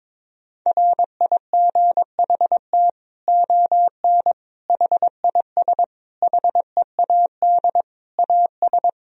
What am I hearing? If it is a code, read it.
Morse code, heard as a keyed tone, about 22 wpm, 700 Hz. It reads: RIGHT ON HIS HEAD AS
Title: Sentences from Top 1000 Words - Review Part 2 - 22wpm